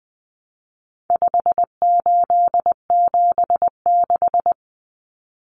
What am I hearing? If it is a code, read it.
Morse code, heard as a keyed tone, about 20 wpm, 700 Hz. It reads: 5876
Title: Numbers - 4-Digit 20wpm